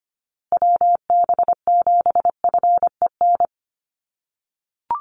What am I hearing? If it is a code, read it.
Morse code, heard as a keyed tone, about 25 wpm, 700 Hz. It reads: WB7FEN
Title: USA Call Signs Repeated 3x 25wpm